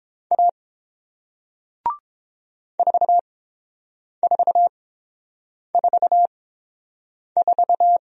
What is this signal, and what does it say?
Morse code, heard as a keyed tone, about 30 wpm, spaced as for 7 wpm, 700 Hz. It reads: A 4444